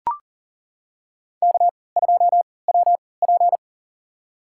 Morse code, heard as a keyed tone, about 40 wpm, spaced as for 23 wpm, 700 Hz. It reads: K2WP
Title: USA Call Signs - 4 Characters - 3x Character Spacing - 40wpm